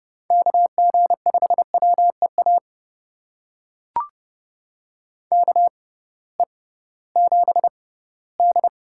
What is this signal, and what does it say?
Morse code, heard as a keyed tone, about 31 wpm, spaced as for 18 wpm, 700 Hz. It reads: KG5WEA KE7D
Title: USA Call Signs - 6 Characters - 6x Character Spacing - 30wpm